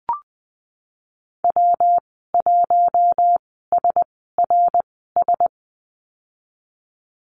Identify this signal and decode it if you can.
Morse code, heard as a keyed tone, about 20 wpm, spaced as for 15 wpm, 700 Hz. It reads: W1SRS
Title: 154 Lesson 25g - Letter p - Call Signs 20wpm